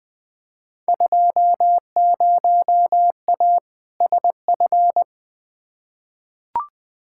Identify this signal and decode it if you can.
Morse code, heard as a keyed tone, about 20 wpm, 700 Hz. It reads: 20A SF
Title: Field Day Exchanges Encore - 20wpm